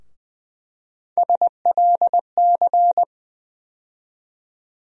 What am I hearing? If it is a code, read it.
Morse code, heard as a keyed tone, about 21 wpm, 700 Hz. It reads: SLC
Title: USA Call Sign Postfixes - 3 Letters 20wpm